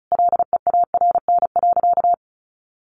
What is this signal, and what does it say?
Morse code, heard as a keyed tone, about 34 wpm, 700 Hz. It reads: LEARN.
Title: Sentences from Top 1400 Words 35wpm